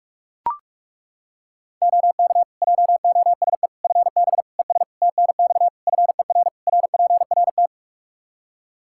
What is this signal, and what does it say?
Morse code, heard as a keyed tone, about 41 wpm, 700 Hz. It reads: OK JOSE FB ES TNX FER RPRT
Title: K1USN SST QSO Practice 45wpm